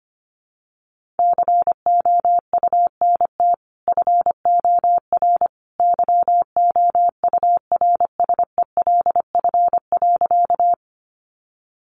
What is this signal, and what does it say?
Morse code, heard as a keyed tone, about 25 wpm, 700 Hz. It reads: COUNT FOR YOURSELF.
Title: Sentences from Top 1500 Words 25wpm